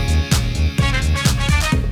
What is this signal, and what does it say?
There is no code.